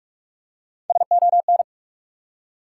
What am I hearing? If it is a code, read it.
Morse code, heard as a keyed tone, about 42 wpm, 700 Hz. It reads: SON
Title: Three Letter Words - 45wpm